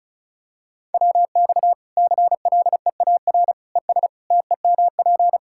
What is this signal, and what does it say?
Morse code, heard as a keyed tone, about 36 wpm, 700 Hz. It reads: WX CLEAR ES TEMP
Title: Practice QSO - Exchange 2 - Info - 35wpm